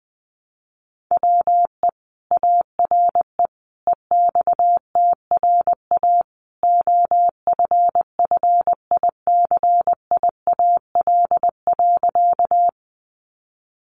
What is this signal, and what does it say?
Morse code, heard as a keyed tone, about 19 wpm, 700 Hz. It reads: WE ARE EXTRA OFFICIAL.
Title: Sentences from Top 1000 Words 20wpm